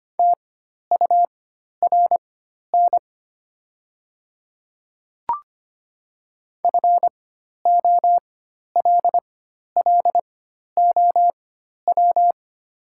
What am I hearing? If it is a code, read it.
Morse code, heard as a keyed tone, about 25 wpm, spaced as for 12 wpm, 700 Hz. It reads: TURN FOLLOW
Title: 4x Character Spacing - Top 200 Words - 25wpm